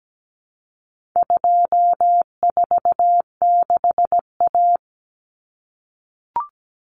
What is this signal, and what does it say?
Morse code, heard as a keyed tone, about 17 wpm, 700 Hz. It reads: 246A
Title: Common Cut Numbers - 4-Digit 17wpm